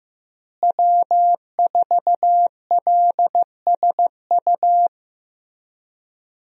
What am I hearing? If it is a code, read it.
Morse code, heard as a keyed tone, about 15 wpm, 700 Hz. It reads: W4LSU
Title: USA Call Signs - 5 Characters - Repeated 3x - 15wpm